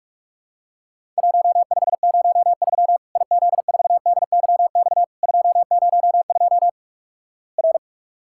Morse code, heard as a keyed tone, about 44 wpm, 700 Hz. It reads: 1503 IZ4DYX 202 R
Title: Worked All Europe DX Contest - QTC Exchange - 45wpm